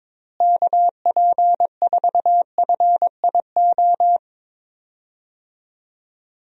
Morse code, heard as a keyed tone, about 21 wpm, 700 Hz. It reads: KP4FIO
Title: USA Call Signs - 6 Characters - Repeated 3x - 22wpm